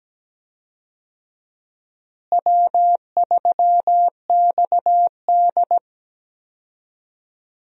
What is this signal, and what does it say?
Morse code, heard as a keyed tone, about 17 wpm, 700 Hz. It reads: W3XD
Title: USA Call Signs - 4 Characters - Repeated 3x - 17wpm